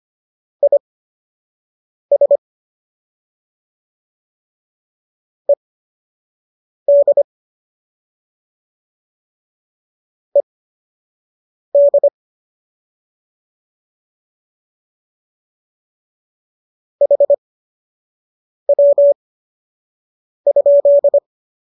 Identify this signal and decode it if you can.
Morse code, heard as a keyed tone, about 25 wpm, spaced as for 6 wpm, 570 Hz. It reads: IS ED ED HW?
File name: qso003 06